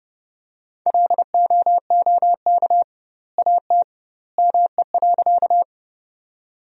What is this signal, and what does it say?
Morse code, heard as a keyed tone, about 32 wpm, 700 Hz. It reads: LOOK AT ME.
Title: 2x Word-Spacing - Sentences from Top 200 Words - 30wpm